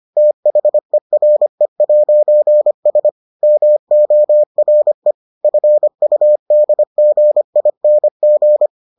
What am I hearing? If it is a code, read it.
Morse code, heard as a keyed tone, about 25 wpm, 600 Hz. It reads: THERE'S MORE FUDGING